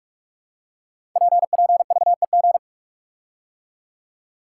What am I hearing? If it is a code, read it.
Morse code, heard as a keyed tone, about 44 wpm, 700 Hz. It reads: PPVEG
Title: Code Groups - 5 Letter:Numbers - 45wpm